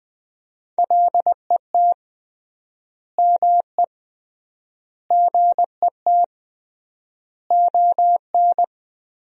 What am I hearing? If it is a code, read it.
Morse code, heard as a keyed tone, about 20 wpm, 700 Hz. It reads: LET ME GET ON
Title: 3x Word-Spacing - Sentences from Top 300 Words - 20wpm